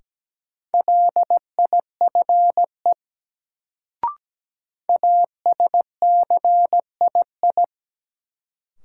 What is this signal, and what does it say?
Morse code, heard as a keyed tone, about 17 wpm, 700 Hz. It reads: LIFE ASCII
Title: Sets of 5 Words - New Word Drill - 17wpm